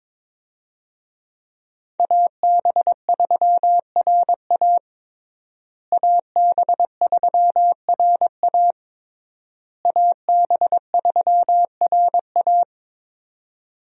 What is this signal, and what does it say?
Morse code, heard as a keyed tone, about 22 wpm, 700 Hz. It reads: AB3RA AB3RA AB3RA
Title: USA Call Signs - 5 Characters - Repeated 3x - 22wpm